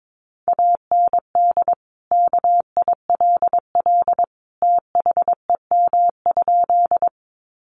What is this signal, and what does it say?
Morse code, heard as a keyed tone, about 22 wpm, 700 Hz. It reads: AND KILL THEM?
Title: Sentences from Top 1400 Words 22wpm